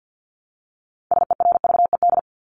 Morse code, heard as a keyed tone, about 50 wpm, 700 Hz. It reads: SERVED
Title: ED Suffix - Top 100 Words with Base Words 50wpm